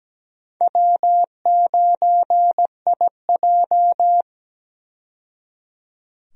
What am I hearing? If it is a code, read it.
Morse code, heard as a keyed tone, about 17 wpm, 700 Hz. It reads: W9IJ
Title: USA Call Signs - 4 Characters - 17wpm